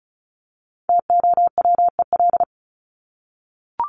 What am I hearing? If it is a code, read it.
Morse code, heard as a keyed tone, about 34 wpm, 700 Hz. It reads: TOWEL
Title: Top 1501-1750 Words 35wpm